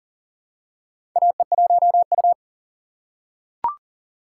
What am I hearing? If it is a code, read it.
Morse code, heard as a keyed tone, about 40 wpm, 700 Hz. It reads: AE1U
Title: USA Call Signs - 4 Characters - 4x Character Spacing - 40wpm